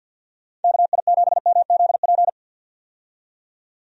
Morse code, heard as a keyed tone, about 51 wpm, 700 Hz. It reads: KI7MZP
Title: USA Call Signs - 6 Characters - 50wpm